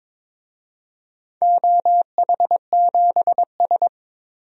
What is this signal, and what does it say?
Morse code, heard as a keyed tone, about 23 wpm, 700 Hz. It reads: OH7S
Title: DXCC-100 Call Signs - 4 Characters - 22wpm